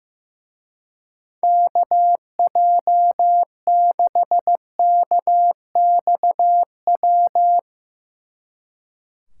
Morse code, heard as a keyed tone, about 16 wpm, 700 Hz. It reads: KJ6KXW
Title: Speed Racing - USA 2x3 Call Signs 15wpm